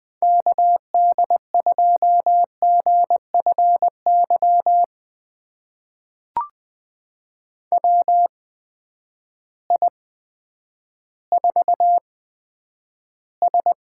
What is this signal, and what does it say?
Morse code, heard as a keyed tone, about 20 wpm, spaced as for 10 wpm, 700 Hz. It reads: KD2GFY WI4S
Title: USA Call Signs - 8x Character Spacing - 20wpm